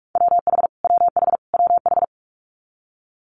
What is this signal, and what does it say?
Morse code, heard as a keyed tone, about 46 wpm, 700 Hz. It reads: WH WH WH
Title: Multi-letter Phonemes Repeated 3x 45wpm